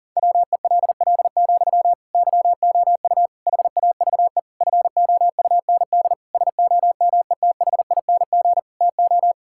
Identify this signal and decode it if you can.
Morse code, heard as a keyed tone, about 40 wpm, 700 Hz. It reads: WELL, YOU HAVE FOUND SOMETHING TO
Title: Sentences from Top 1000 Words - Review Part 6 - 40wpm